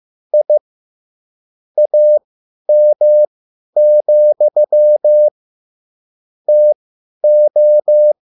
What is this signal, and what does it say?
Morse code, heard as a keyed tone, about 15 wpm, spaced as for 10 wpm, 600 Hz. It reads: I AM, TO